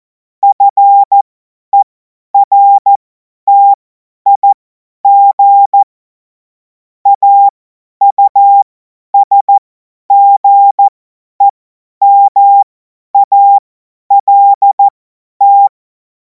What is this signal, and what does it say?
Morse code, heard as a keyed tone, about 14 wpm, spaced as for 10 wpm, 800 Hz. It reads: FERTIG AUSGEMALT
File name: Morsekurs / Witze / Witz 05. Mai 2026 50 F70